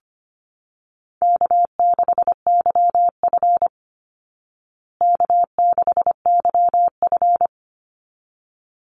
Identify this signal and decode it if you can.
Morse code, heard as a keyed tone, about 25 wpm, 700 Hz. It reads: K6YF K6YF
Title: Easy Call Signs Repeated 2x - 25wpm